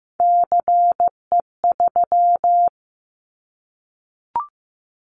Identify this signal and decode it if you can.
Morse code, heard as a keyed tone, about 15 wpm, 700 Hz. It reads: CE3
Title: Callsign Prefixes for Top 100 DXCC - 15wpm